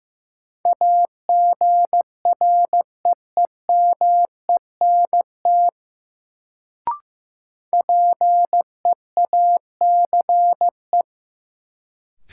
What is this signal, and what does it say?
Morse code, heard as a keyed tone, about 15 wpm, 700 Hz. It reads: AGREEMENT PEACE
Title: Sentences from Top 1500 Words - New Word Drill Alt - 15wpm